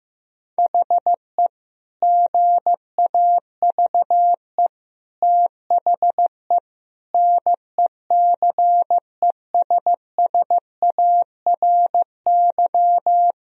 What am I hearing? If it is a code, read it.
Morse code, heard as a keyed tone, about 16 wpm, 700 Hz. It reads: HE GAVE THE NECESSARY
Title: Sentences from Top 1000 Words - Review Part 2 - 15wpm